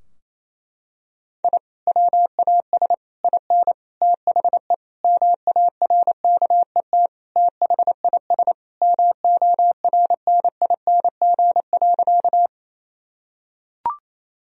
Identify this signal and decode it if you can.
Morse code, heard as a keyed tone, about 28 wpm, 700 Hz. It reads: I WAS IN THE MARKET THIS MORNING.
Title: Sentences from Top 1000 Words - Review Part 3 - 28wpm